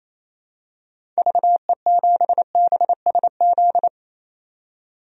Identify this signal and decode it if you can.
Morse code, heard as a keyed tone, about 28 wpm, 700 Hz. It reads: VE7BSZ
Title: DXCC-100 Call Signs - 6 Characters - 28wpm